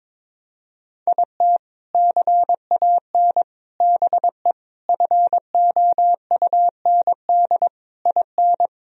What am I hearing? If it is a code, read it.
Morse code, heard as a keyed tone, about 21 wpm, 700 Hz. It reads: IT CAN BE FOUND IN
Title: Sentences from Top 1000 Words - Review Part 1 - 22wpm